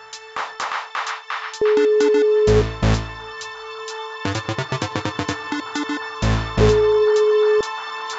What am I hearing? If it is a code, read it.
Ringback tone (Europe, 425 Hz), with noise.